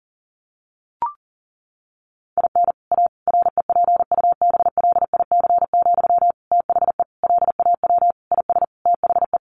Morse code, heard as a keyed tone, about 40 wpm, 700 Hz. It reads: IN A REPUBLIC, THE LAW IS THE
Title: Sentences from Top 1500 Words 40wpm